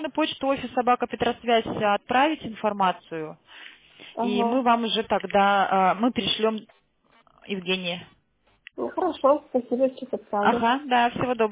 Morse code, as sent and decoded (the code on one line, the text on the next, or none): none